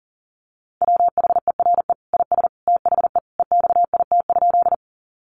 Morse code, heard as a keyed tone, about 40 wpm, 700 Hz. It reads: WHERE IS THE EXIT?
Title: Sentences from Top 1100 Words 40wpm